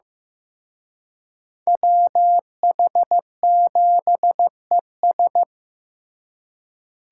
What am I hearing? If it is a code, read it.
Morse code, heard as a keyed tone, about 18 wpm, 700 Hz. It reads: WH7ES
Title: Speed Racing - USA 2x2 Call Signs 15wpm